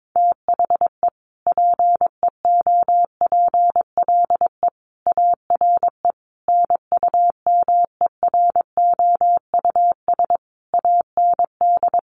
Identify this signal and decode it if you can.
Morse code, heard as a keyed tone, about 22 wpm, 700 Hz. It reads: THE PEOPLE ARE NUMEROUS AND
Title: Sentences from Top 1400 Words 22wpm